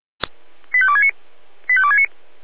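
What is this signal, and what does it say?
Five-tone selective call (ZVEI), heard as quick runs of five tones, tones of about 70 ms. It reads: 85269 85269
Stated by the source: ZVEI five-tone selective call 85810, received by SDR, logged by Feuerwehr Orlamünde (f85360).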